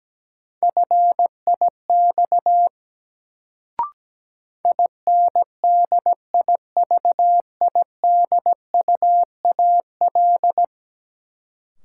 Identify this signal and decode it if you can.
Morse code, heard as a keyed tone, about 17 wpm, 700 Hz. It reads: FIX INDIVIDUAL
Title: Sentences from Top 1400 Words - New Word Drill Alt - 17wpm